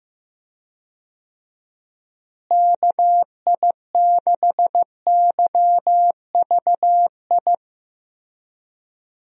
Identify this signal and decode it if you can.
Morse code, heard as a keyed tone, about 15 wpm, 700 Hz. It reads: KI6YVI
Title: USA Call Signs - 6 Characters - Repeated 2x - 15wpm